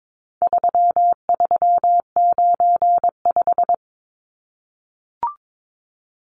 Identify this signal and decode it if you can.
Morse code, heard as a keyed tone, about 22 wpm, 700 Hz. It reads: 3395
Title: Numbers - 4-Digit 22wpm